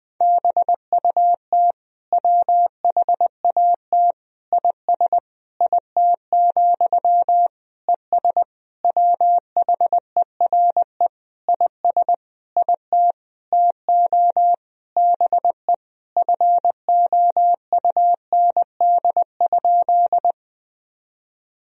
Morse code, heard as a keyed tone, about 20 wpm, 700 Hz. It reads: BUT WHAT IS IT, ES WHERE IS IT TO BE FOUND?
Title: Sentences from Top 300 Words and Abbreviations 20wpm